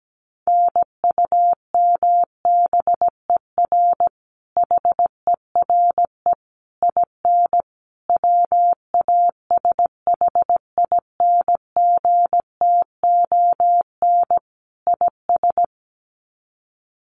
Morse code, heard as a keyed tone, about 17 wpm, 700 Hz. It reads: NUMBER HERE IN WASHINGTON IS
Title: Sets of 5 Words Encore2 17wpm